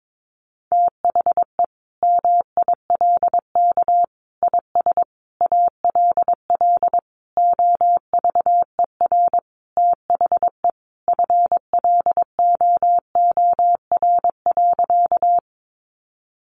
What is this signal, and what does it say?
Morse code, heard as a keyed tone, about 22 wpm, 700 Hz. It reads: THE MILK IS ALL OVER THE FLOOR.
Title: Additional Easy Phrases 22wpm